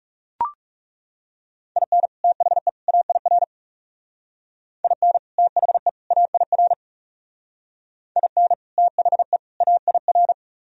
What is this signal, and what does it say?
Morse code, heard as a keyed tone, about 39 wpm, 700 Hz. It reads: IN THE AIR IN THE AIR IN THE AIR